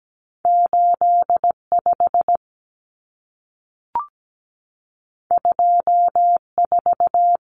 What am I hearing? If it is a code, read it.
Morse code, heard as a keyed tone, about 17 wpm, 700 Hz. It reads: 85 24
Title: Common Cut Numbers - 2-Digit 17wpm